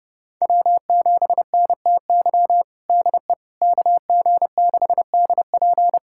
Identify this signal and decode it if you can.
Morse code, heard as a keyed tone, about 31 wpm, 700 Hz. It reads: W7NTY DE KG6DP
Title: USA Callsign DE Callsign - 5 Characters - 30wpm